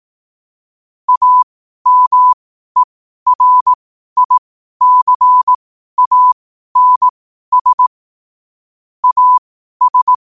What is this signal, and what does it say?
Morse code, heard as a keyed tone, about 18 wpm, spaced as for 12 wpm, 1000 Hz. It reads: AMERICANS AS
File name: qotdfiles / 12wpm